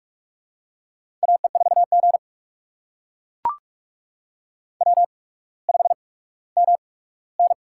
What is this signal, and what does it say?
Morse code, heard as a keyed tone, about 44 wpm, spaced as for 19 wpm, 700 Hz. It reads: AE4G W5MN
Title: USA Call Signs - 4 Characters - 8x Character Spacing - 45wpm